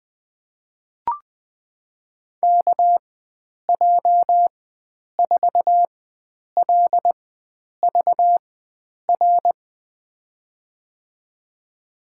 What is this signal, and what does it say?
Morse code, heard as a keyed tone, about 21 wpm, spaced as for 9 wpm, 700 Hz. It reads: KJ4LVR
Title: USA Call Signs - 6 Characters - 4x Character Spacing - 20wpm